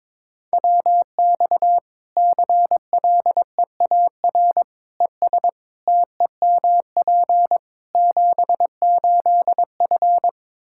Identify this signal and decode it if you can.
Morse code, heard as a keyed tone, about 21 wpm, 700 Hz. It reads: WX CLEAR ES TEMP 78F
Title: Practice QSO - Exchange 2 - Info - 22wpm